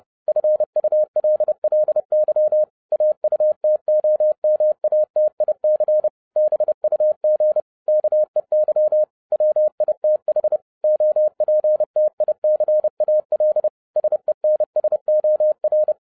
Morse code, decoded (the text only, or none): FULLY AUTOMATIC BUG KEY WITH OPTICAL SENSOR